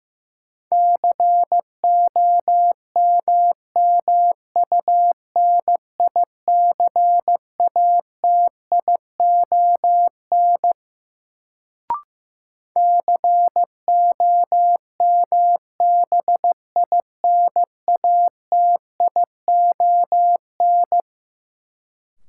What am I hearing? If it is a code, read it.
Morse code, heard as a keyed tone, about 15 wpm, 700 Hz. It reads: COMMUNICATION COMBINATION
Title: Sentences from Top 1000 Words - Review Part 6 - New Word Drill - 15wpm